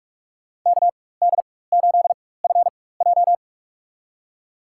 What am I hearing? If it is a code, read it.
Morse code, heard as a keyed tone, about 46 wpm, spaced as for 21 wpm, 700 Hz. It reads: KD8FJ
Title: USA Call Signs - 4x Character Spacing - 45wpm